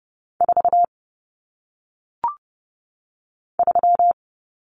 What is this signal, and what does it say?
Morse code, heard as a keyed tone, about 30 wpm, 700 Hz. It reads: 4 3
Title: Numbers - 1-Digit 30wpm